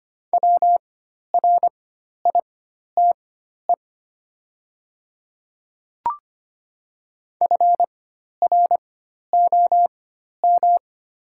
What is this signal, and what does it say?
Morse code, heard as a keyed tone, about 25 wpm, spaced as for 12 wpm, 700 Hz. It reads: WRITE FROM
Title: 4x Character Spacing - Top 100 Words - 25wpm